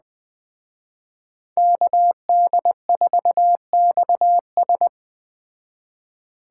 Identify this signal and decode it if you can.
Morse code, heard as a keyed tone, about 20 wpm, 700 Hz. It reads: KD4XS
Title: Speed Racing - USA 2x2 Call Signs 17wpm